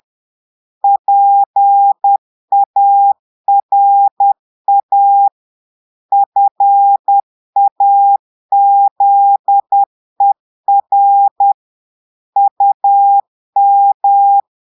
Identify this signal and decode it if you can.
Morse code, heard as a keyed tone, about 10 wpm, 800 Hz. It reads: PARA FAZER UM
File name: portugais texte en clair 2 vitesse 10 mots minutes